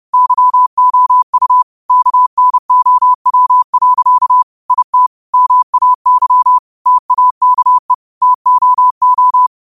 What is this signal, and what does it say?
Morse code, heard as a keyed tone, about 30 wpm, 1000 Hz. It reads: YOU KNOW. IT MAY TAKE TOO